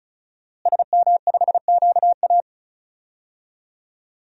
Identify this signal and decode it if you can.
Morse code, heard as a keyed tone, about 36 wpm, 700 Hz. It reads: SM5QA
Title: DXCC-100 Call Signs - 5 Characters - Repeated 2x - 35wpm